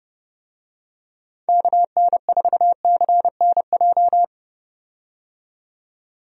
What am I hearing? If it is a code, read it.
Morse code, heard as a keyed tone, about 30 wpm, 700 Hz. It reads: KN4CNJ
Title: USA Call Signs - 6 Characters - 30wpm